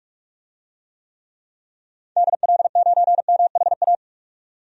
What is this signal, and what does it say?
Morse code, heard as a keyed tone, about 42 wpm, 700 Hz. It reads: DL9MHA
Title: DXCC-100 Call Signs - 6 Characters - 45wpm